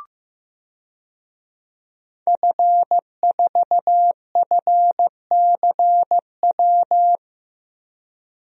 Morse code, heard as a keyed tone, about 15 wpm, 700 Hz. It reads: F4FCW
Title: DXCC-100 Call Signs - 5 Characters - Repeated 3x - 15wpm